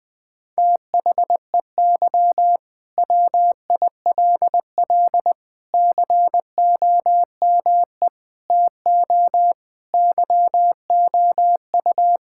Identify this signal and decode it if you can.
Morse code, heard as a keyed tone, about 20 wpm, 700 Hz. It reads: THEY WILL COME TO YOU
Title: Sentences from Top 1600 Words 20wpm